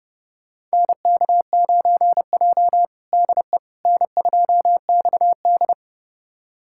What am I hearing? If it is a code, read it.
Morse code, heard as a keyed tone, about 32 wpm, 700 Hz. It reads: NK9J DE N2XD
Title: USA Callsign DE Callsign - 4 Characters - 30wpm